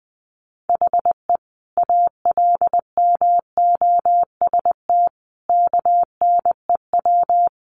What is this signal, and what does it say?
Morse code, heard as a keyed tone, about 20 wpm, 700 Hz. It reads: HE ALMOST KNEW
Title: Sentences from Top 700 Words 20wpm